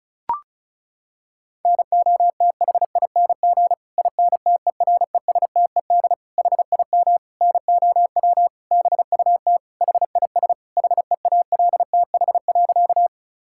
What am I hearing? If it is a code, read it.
Morse code, heard as a keyed tone, about 34 wpm, 700 Hz. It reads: NOTHING INTERESTED HIM NOW BUT HIS HEALTH.
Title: Sentences from Top 1000 Words - Review Part 6 - 35wpm